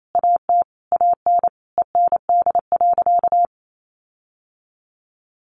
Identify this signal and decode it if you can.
Morse code, heard as a keyed tone, about 28 wpm, 700 Hz. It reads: AT AN END.
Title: Sentences from Top 1500 Words 28wpm